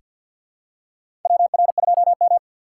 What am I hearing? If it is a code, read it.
Morse code, heard as a keyed tone, about 48 wpm, 700 Hz. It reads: WR2M
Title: USA Call Signs - 4x Character Spacing - 50wpm